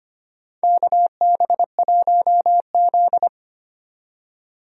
Morse code, heard as a keyed tone, about 25 wpm, 700 Hz. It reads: KB1Z
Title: USA Call Signs - 4 Characters - 25wpm